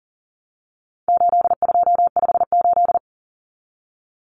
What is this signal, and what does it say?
Morse code, heard as a keyed tone, about 40 wpm, 700 Hz. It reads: 8258
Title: Common Cut Numbers - 4-Digit 40wpm